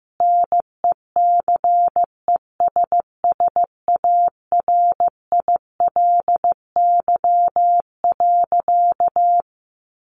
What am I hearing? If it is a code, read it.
Morse code, heard as a keyed tone, about 15 wpm, 700 Hz. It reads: NECESSARILY.
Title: Sentences from Top 1400 Words 15wpm